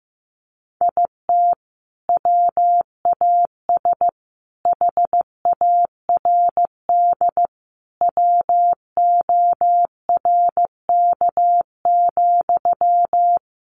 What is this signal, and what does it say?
Morse code, heard as a keyed tone, about 15 wpm, 700 Hz. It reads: IT WAS HARD WORK,
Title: Sentences from Top 600 Words 15wpm